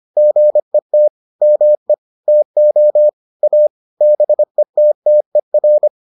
Morse code, heard as a keyed tone, about 25 wpm, 600 Hz. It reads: GET ME TO A BETTER